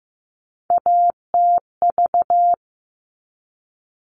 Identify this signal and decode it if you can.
Morse code, heard as a keyed tone, about 15 wpm, 700 Hz. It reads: ATV
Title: All Cut Numbers - 3-Digit 15wpm